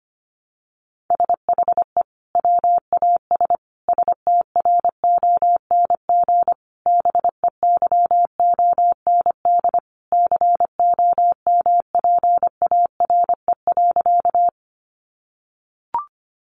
Morse code, heard as a keyed tone, about 25 wpm, 700 Hz. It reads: SHE WAS STRONG BEYOND COMPARE.
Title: Sentences from Top 1300 Words 25wpm